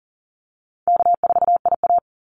Morse code, heard as a keyed tone, about 40 wpm, 700 Hz. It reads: K4IA
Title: Easy Call Signs Repeated 3x - 40wpm